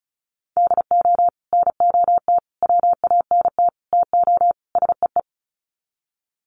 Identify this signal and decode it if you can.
Morse code, heard as a keyed tone, about 34 wpm, 700 Hz. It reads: DO NOT WANT TO SEE
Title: Sets of 5 Words 35wpm